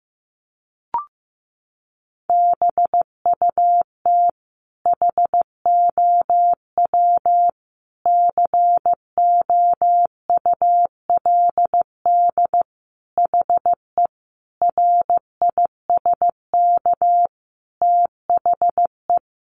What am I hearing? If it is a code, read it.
Morse code, heard as a keyed tone, about 15 wpm, 700 Hz. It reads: BUT HOW COULD HE RISK THE
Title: Sentences from Top 600 Words 15wpm